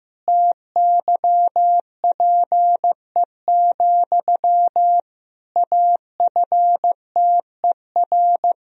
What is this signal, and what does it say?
Morse code, heard as a keyed tone, about 16 wpm, 700 Hz. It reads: TYPE, AFTER
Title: Sentences from Top 1000 Words - Review Part 3 - 15wpm